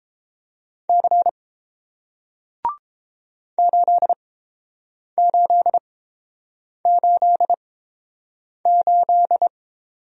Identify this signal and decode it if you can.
Morse code, heard as a keyed tone, about 30 wpm, spaced as for 7 wpm, 700 Hz. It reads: C 8888